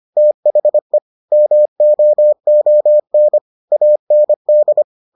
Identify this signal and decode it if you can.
Morse code, heard as a keyed tone, about 25 wpm, 600 Hz. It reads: THE MOON AND